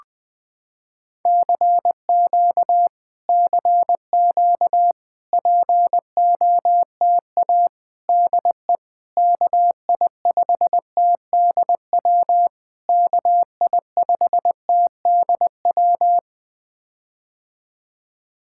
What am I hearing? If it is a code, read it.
Morse code, heard as a keyed tone, about 20 wpm, 700 Hz. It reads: CQ CQ POTA DE KI5TDW KI5TDW
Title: POTA QSO Practice - 20wpm